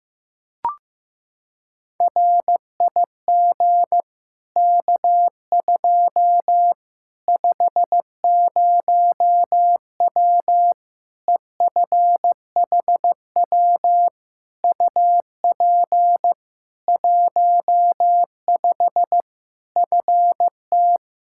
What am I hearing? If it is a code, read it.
Morse code, heard as a keyed tone, about 15 wpm, 700 Hz. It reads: RIG K2 50W EFHW UP 15 FT
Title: Practice QSO - Exchange 2 - Info - 15wpm